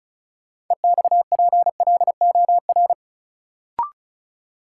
Morse code, heard as a keyed tone, about 34 wpm, 700 Hz. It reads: EXPLOR
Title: Sentences from Top 1100 Words - New Word Drill Alt - 35wpm